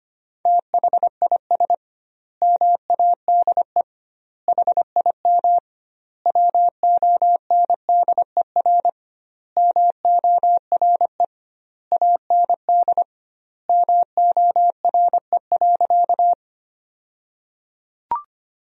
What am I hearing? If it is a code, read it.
Morse code, heard as a keyed tone, about 25 wpm, 700 Hz. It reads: THIS MADE HIM WONDER MORE AND MORE.
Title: 2x Word-Spacing - Sentences from Top 500 Words - 25wpm